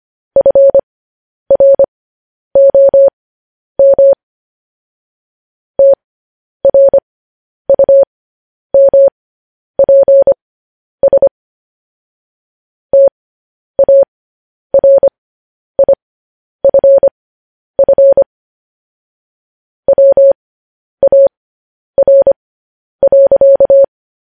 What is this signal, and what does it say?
Morse code, heard as a keyed tone, about 25 wpm, spaced as for 10 wpm, 550 Hz. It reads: FROM TRUMPS TARIFF WAR.
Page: news headlines in morse code at 10 wpm